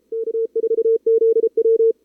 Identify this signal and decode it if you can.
Morse code, heard as a keyed tone, about 29 wpm, 440 Hz. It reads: K4ZW
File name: CWSprint / Feb2022 / k5tr-feb-na-sprint-cw-k5ot -0000